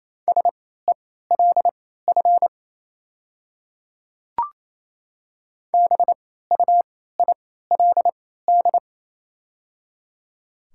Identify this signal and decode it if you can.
Morse code, heard as a keyed tone, about 28 wpm, spaced as for 16 wpm, 700 Hz. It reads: SELF BUILD
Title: 3x Character Spacing - Top 200 Words - 28wpm